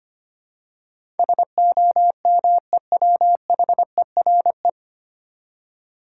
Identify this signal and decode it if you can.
Morse code, heard as a keyed tone, about 25 wpm, 700 Hz. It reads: SOMEWHERE
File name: practice / 25wpm / 9 Letter Words - First 3 Letters Given - 25wpm